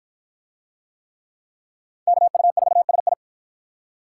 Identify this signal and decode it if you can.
Morse code, heard as a keyed tone, about 47 wpm, 700 Hz. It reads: KU4SI